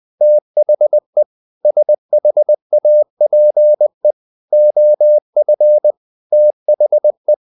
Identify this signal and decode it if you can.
Morse code, heard as a keyed tone, about 20 wpm, 600 Hz. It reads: THE SHAPE OF THE
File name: the old man and the sea / the old man and the sea part 1 20wpm